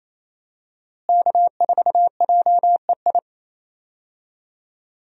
Morse code, heard as a keyed tone, about 28 wpm, 700 Hz. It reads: K4JEI
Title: USA Call Signs - 5 Characters - Repeated 2x - 28wpm